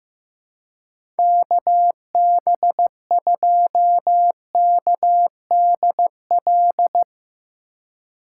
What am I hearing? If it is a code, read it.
Morse code, heard as a keyed tone, about 16 wpm, 700 Hz. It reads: KB2KDL
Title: USA Call Signs - 6 Characters - 4x Character Spacing - 15wpm